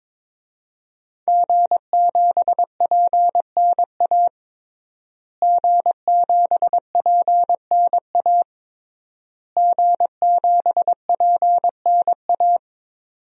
Morse code, heard as a keyed tone, about 22 wpm, 700 Hz. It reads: G7PNA G7PNA G7PNA
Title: DXCC-100 Call Signs - 5 Characters - Repeated 3x - 22wpm